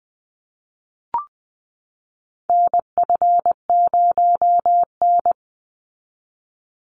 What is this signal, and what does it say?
Morse code, heard as a keyed tone, about 20 wpm, 700 Hz. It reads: NF0N
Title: Call Signs Repeated 3x - 20wpm